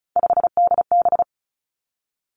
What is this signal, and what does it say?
Morse code, heard as a keyed tone, about 36 wpm, 700 Hz. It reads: 5DB
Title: Three Challenging Letter-Number Combinations 35wpm